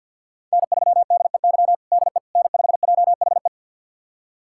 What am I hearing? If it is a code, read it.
Morse code, heard as a keyed tone, about 52 wpm, 700 Hz. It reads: N3DEY DE N5JHE
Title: USA Callsign DE Callsign - 5 Characters - 50wpm